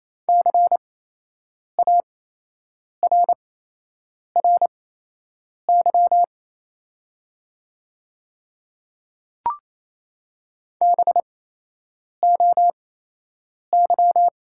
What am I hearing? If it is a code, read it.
Morse code, heard as a keyed tone, about 28 wpm, spaced as for 8 wpm, 700 Hz. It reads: CARRY BOY
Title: 8x Character Spacing - Top 300 Words - 28wpm